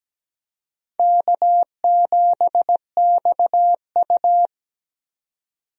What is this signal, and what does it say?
Morse code, heard as a keyed tone, about 17 wpm, 700 Hz. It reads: K7XU
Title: USA Call Signs - 4 Characters - 17wpm